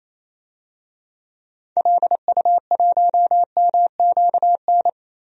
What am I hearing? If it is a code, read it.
Morse code, heard as a keyed tone, about 28 wpm, 700 Hz. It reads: LU1MQN
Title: DXCC-100 Call Signs - 6 Characters - 28wpm